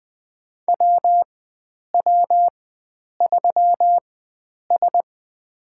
Morse code, heard as a keyed tone, about 20 wpm, spaced as for 9 wpm, 700 Hz. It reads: WW3S
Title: USA Call Signs - 4 Characters - 4x Character Spacing - 20wpm